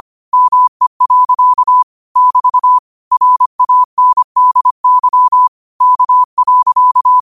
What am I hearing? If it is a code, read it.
Morse code, heard as a keyed tone, about 25 wpm, 1000 Hz. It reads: ME. <BT> RANDY K.
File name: 25wpm